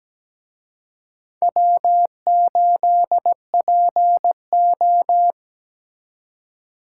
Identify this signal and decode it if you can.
Morse code, heard as a keyed tone, about 20 wpm, 700 Hz. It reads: W8PO
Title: Speed Racing - USA 1x2 Call Signs 17wpm